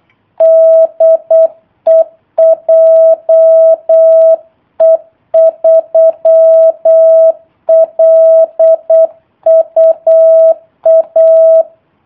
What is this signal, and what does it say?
Morse code, heard as a keyed tone, about 8 wpm, 650 Hz. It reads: DE JE3LUA
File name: jh8cbh / 2019cw / sp03- w